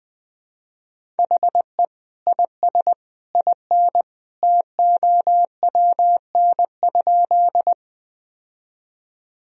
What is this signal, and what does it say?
Morse code, heard as a keyed tone, about 21 wpm, 700 Hz. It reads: HE IS IN TOWN?
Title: Sentences from Top 1000 Words - Review Part 2 - 20wpm